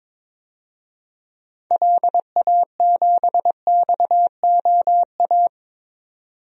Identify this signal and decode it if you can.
Morse code, heard as a keyed tone, about 22 wpm, 700 Hz. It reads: LA7XOA
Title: DXCC-100 Call Signs - 6 Characters - 22wpm